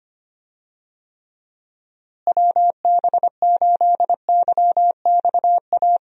Morse code, heard as a keyed tone, about 25 wpm, 700 Hz. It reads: WB8YXA
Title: USA Call Signs - 6 Characters - Repeated 2x - 25wpm